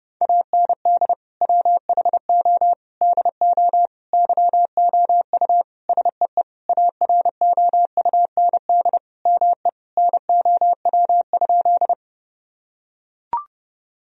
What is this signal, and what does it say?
Morse code, heard as a keyed tone, about 32 wpm, 700 Hz. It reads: AND WHO DO YOU SEE AROUND ME NOW?
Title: Sentences from Top 1000 Words - Review Part 3 - 30wpm